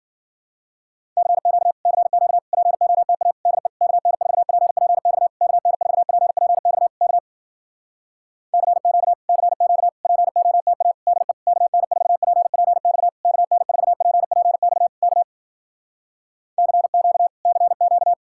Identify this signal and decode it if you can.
Morse code, heard as a keyed tone, about 53 wpm, 700 Hz. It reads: CQ CQ POTA DE KN4PPX KN4PPX K CQ CQ POTA DE KN4PPX KN4PPX K CQ CQ